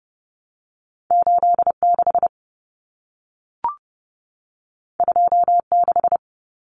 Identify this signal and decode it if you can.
Morse code, heard as a keyed tone, about 30 wpm, 700 Hz. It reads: 86 26
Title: Numbers - 2-Digit 30wpm